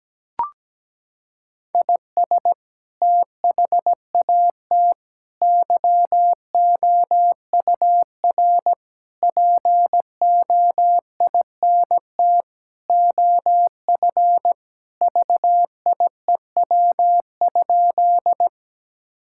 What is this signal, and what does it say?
Morse code, heard as a keyed tone, about 17 wpm, 700 Hz. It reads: IS THAT YOUR POINT OF VIEW?
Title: Sentences from Top 1000 Words - Review Part 6 - 17wpm